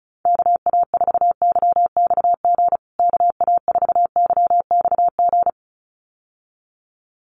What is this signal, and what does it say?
Morse code, heard as a keyed tone, about 34 wpm, 700 Hz. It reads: KA4YXG KA4YXG
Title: USA Call Signs Repeated 2x 35wpm